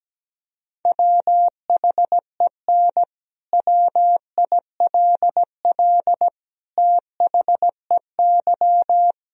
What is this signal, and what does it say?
Morse code, heard as a keyed tone, about 17 wpm, 700 Hz. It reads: WHEN WILL THEY
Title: Sentences from Top 1000 Words - Review Part 4 - 17wpm